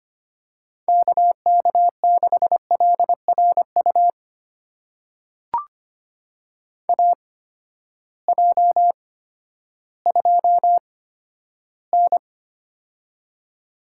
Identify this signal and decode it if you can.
Morse code, heard as a keyed tone, about 25 wpm, spaced as for 12 wpm, 700 Hz. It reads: KK6LRU AJ2N
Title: USA Call Signs - 8x Character Spacing - 25wpm